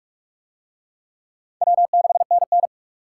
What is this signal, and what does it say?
Morse code, heard as a keyed tone, about 48 wpm, 700 Hz. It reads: W6NN